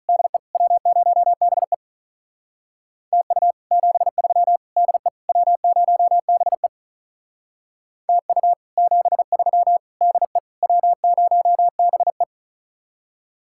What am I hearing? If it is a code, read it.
Morse code, heard as a keyed tone, about 42 wpm, 700 Hz. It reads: DE W0BE TU 73 DE W0BE TU 73 DE W0BE
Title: Speed-Racing Common QSO Phrases Encore - 35wpm